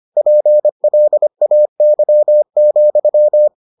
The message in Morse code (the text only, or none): PLAY,